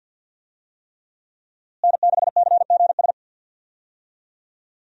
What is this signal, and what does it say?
Morse code, heard as a keyed tone, about 52 wpm, 700 Hz. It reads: N6CGS
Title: USA Call Signs - 5 Characters - Repeated 3x - 50wpm